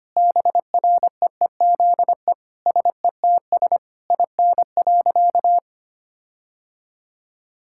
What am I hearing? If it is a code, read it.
Morse code, heard as a keyed tone, about 25 wpm, 700 Hz. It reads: BREEZE SETS IN.
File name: practice / 25wpm / Difficult Sentences with Hints - 25wpm